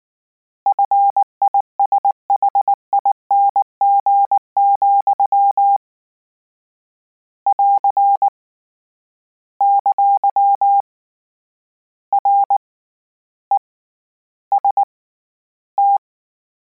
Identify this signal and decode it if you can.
Morse code, heard as a keyed tone, about 19 wpm, 800 Hz. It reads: FISHING, <AR> ! R E S T
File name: Combinations / Combinations-mp3s / Combinations@18